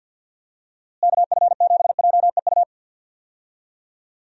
Morse code, heard as a keyed tone, about 48 wpm, 700 Hz. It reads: KF7JEU
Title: USA Call Signs - 6 Characters - Repeated 3x - 50wpm